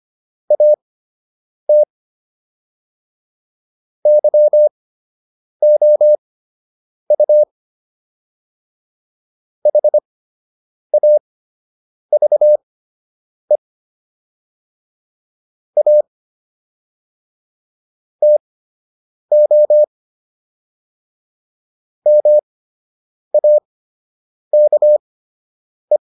Morse code, heard as a keyed tone, about 25 wpm, spaced as for 8 wpm, 600 Hz. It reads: AT YOU HAVE A TO MAKE